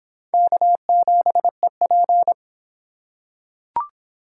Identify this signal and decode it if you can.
Morse code, heard as a keyed tone, about 22 wpm, 700 Hz. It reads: K7EP